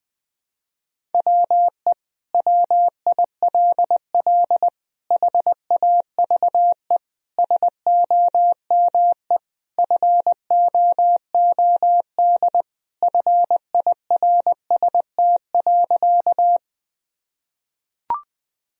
Morse code, heard as a keyed tone, about 21 wpm, 700 Hz. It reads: WE WILL HAVE SOME FOOD FIRST.
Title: Sentences from Top 1000 Words - Review Part 1 - 20wpm